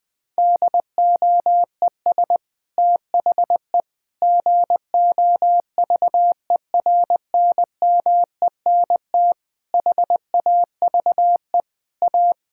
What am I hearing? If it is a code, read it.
Morse code, heard as a keyed tone, about 21 wpm, 700 Hz. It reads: DOES THE GOVERNMENT HAVE A
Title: Sentences from Top 1600 Words 20wpm